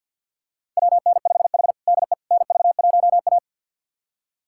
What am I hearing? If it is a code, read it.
Morse code, heard as a keyed tone, about 48 wpm, 700 Hz. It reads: WN5H DE NV1A
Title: USA Callsign DE Callsign - 4 Characters - 50wpm